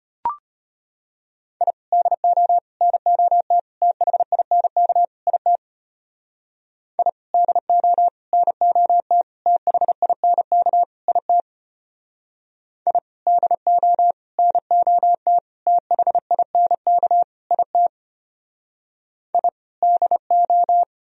Morse code, heard as a keyed tone, about 33 wpm, 700 Hz. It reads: I DO NOT THINK IT I DO NOT THINK IT I DO NOT THINK IT I DO